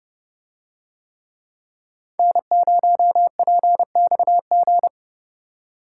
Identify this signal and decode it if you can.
Morse code, heard as a keyed tone, about 29 wpm, 700 Hz. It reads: N0PXG